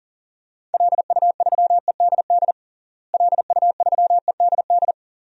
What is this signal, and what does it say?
Morse code, heard as a keyed tone, about 40 wpm, 700 Hz. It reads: LU3EDD LU3EDD
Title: DXCC-100 Call Signs - 6 Characters - Repeated 2x - 40wpm